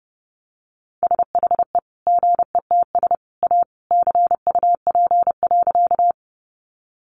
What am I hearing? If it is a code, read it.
Morse code, heard as a keyed tone, about 30 wpm, 700 Hz. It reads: SHE GETS A CUP.
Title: Easy Phrases 30wpm